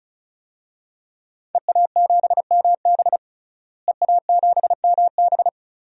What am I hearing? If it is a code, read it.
Morse code, heard as a keyed tone, about 36 wpm, 700 Hz. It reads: EA7MB EA7MB
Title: DXCC-100 Call Signs - 5 Characters - Repeated 2x - 35wpm